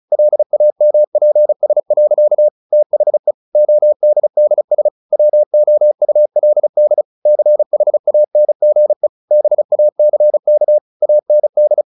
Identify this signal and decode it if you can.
Morse code, heard as a keyed tone, about 34 wpm, 600 Hz. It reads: LAMPS. THE ODDS WOULD CHANGE BACK AND
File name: the old man and the sea 2 35wpm